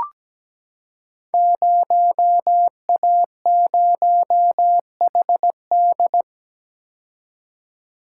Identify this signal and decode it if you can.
Morse code, heard as a keyed tone, about 17 wpm, 700 Hz. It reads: 0A0HD
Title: Code Groups - 5 Letter:Numbers - 17wpm